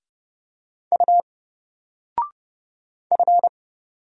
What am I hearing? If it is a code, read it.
Morse code, heard as a keyed tone, about 30 wpm, 700 Hz. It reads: U F
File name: Single-Letter-Number-30wpm